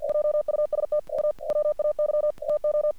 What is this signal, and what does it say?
Morse code, heard as a keyed tone, about 48 wpm, 600 Hz. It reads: JUST A WAY TO
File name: archived / CRSnet